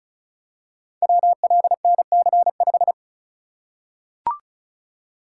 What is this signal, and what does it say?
Morse code, heard as a keyed tone, about 34 wpm, 700 Hz. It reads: WLNC5
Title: Code Groups - 5 Letter:Numbers - 35wpm